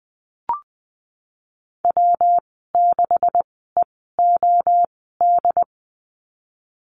Morse code, heard as a keyed tone, about 20 wpm, spaced as for 15 wpm, 700 Hz. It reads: W6EOD
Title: 173 Lesson 28g - Number 9 - Call Signs 20wpm